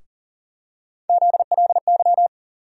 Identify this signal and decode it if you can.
Morse code, heard as a keyed tone, about 40 wpm, 700 Hz. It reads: ZLY
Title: USA Call Sign Postfixes - 3 Letters 40wpm